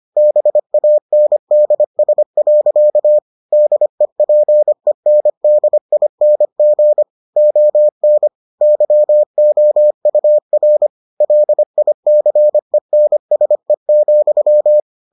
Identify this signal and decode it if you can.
Morse code, heard as a keyed tone, about 25 wpm, 600 Hz. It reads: BANDS. DEPENDING ON YOUR LICENSE,